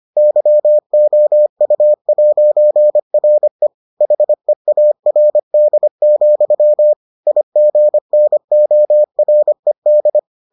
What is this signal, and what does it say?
Morse code, heard as a keyed tone, about 25 wpm, 600 Hz. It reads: YOU'RE HEARD, IGNORED